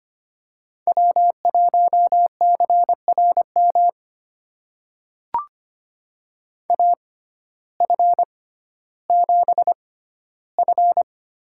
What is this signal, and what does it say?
Morse code, heard as a keyed tone, about 25 wpm, spaced as for 14 wpm, 700 Hz. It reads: W1CRM AF7F
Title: USA Call Signs - 5 Characters - 6x Character Spacing - 25wpm